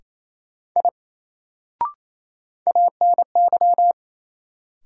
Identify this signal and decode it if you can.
Morse code, heard as a keyed tone, about 28 wpm, 700 Hz. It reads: I ANY
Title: Sentences from Top 200 Words - New Word Drill - 28wpm